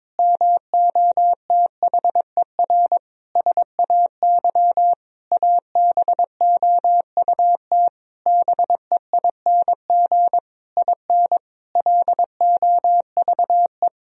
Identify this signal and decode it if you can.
Morse code, heard as a keyed tone, about 22 wpm, 700 Hz. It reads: MOTHER SAY ABOUT BEING IN LOVE
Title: Sentences from Top 1000 Words - Review Part 3 - 22wpm